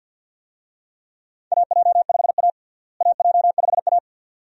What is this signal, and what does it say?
Morse code, heard as a keyed tone, about 53 wpm, 700 Hz. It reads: AJ5A AJ5A
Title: USA Call Signs - 4 Characters - Repeated 2x - 50wpm